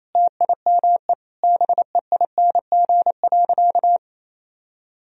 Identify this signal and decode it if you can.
Morse code, heard as a keyed tone, about 28 wpm, 700 Hz. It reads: TIME BEING.
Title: Sentences from Top 1000 Words - Review Part 4 - 28wpm